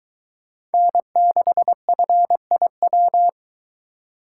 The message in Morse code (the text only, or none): N6FIW